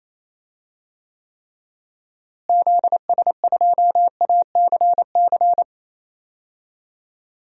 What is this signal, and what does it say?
Morse code, heard as a keyed tone, about 28 wpm, 700 Hz. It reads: ZS2ACC
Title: DXCC-100 Call Signs - 6 Characters - Repeated 3x - 28wpm